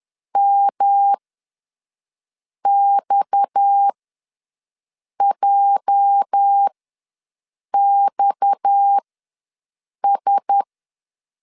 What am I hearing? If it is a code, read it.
Morse code, heard as a keyed tone, about 10 wpm, spaced as for 5 wpm, 800 Hz. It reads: MXJXS
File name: LEZIONI / 8 YJ45 / 8 YJ ws 10wpm 4043